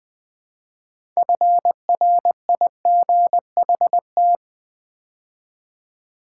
Morse code, heard as a keyed tone, about 20 wpm, 690 Hz. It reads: FRIGHT
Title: Build A Word - 2 Letters At A Time - 8-Letter Words - 20wpm